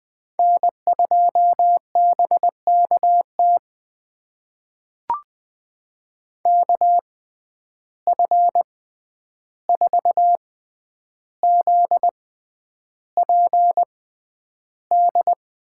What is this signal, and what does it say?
Morse code, heard as a keyed tone, about 21 wpm, spaced as for 10 wpm, 700 Hz. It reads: N2BKT KF4ZPD
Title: USA Call Signs - 6x Character Spacing - 20wpm